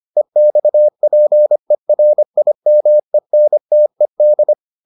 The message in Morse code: EXPERIMENTED

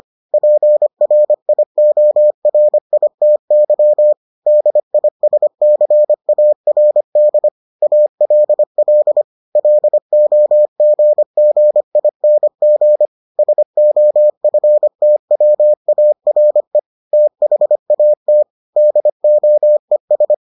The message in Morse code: PRIORITY DISCARD ALL LOGGING SOFTWARE THAT DOES

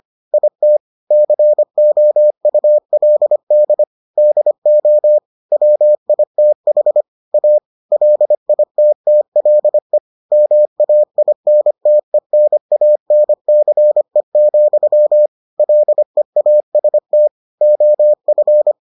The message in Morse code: IT COULD DO WITH A LITTLE MAINTENANCE, LEAST OF

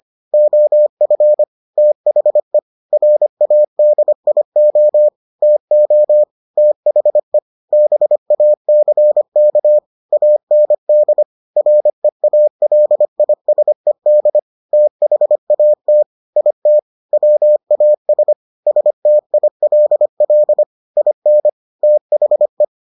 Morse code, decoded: OF THE RADIO TO THE BACK AND REALISED THAT IT WAS STILL IN THE

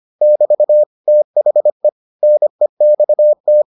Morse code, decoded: <BT> THE NEXT

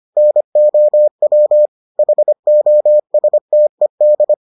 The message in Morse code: NOW HOSTED

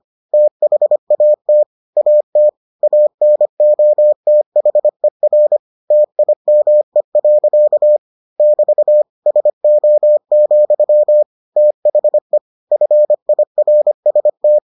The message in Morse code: THAT AT ANOTHER TIME. <BT> SO, THE FIRST

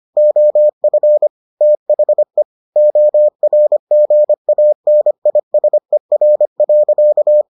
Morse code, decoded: OF THE ORGANISER.

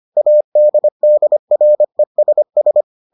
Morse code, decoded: ADDRESS

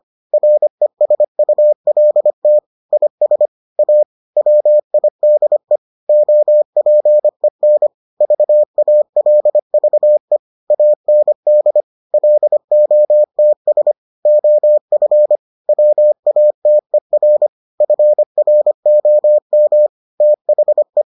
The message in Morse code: RESULT IS A WIDE OPEN VALVE AND LOTS OF WATER FROM THE

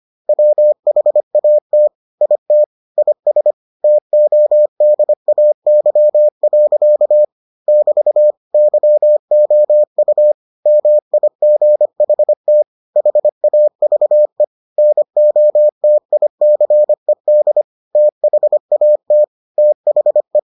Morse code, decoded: WHAT IT IS TODAY. <BT> YOU MIGHT HAVE NOTICED THAT THE

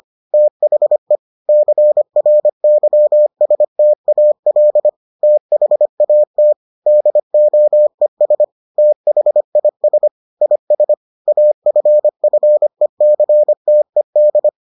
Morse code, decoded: THE CRYSTAL THAT DOES THIS IS AFFECTED

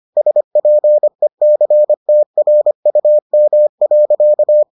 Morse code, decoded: SPECTRUM.

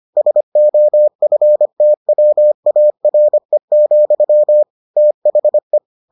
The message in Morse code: SOFTWARE, THE

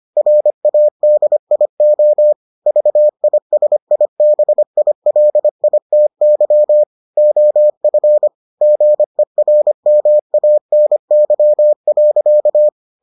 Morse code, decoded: RADIO VISIBILITY OF GERMANY.

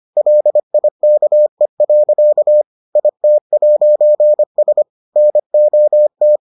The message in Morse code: LIKE. IT'S NOT

